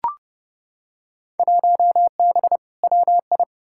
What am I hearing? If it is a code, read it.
Morse code, heard as a keyed tone, about 30 wpm, 700 Hz. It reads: 1B WI
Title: Field Day Exchanges - 30wpm